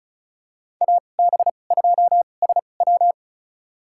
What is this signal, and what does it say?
Morse code, heard as a keyed tone, about 36 wpm, spaced as for 25 wpm, 700 Hz. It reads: AB2SW
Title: USA Call Signs - 5 Characters - 2x Character Spacing - 35wpm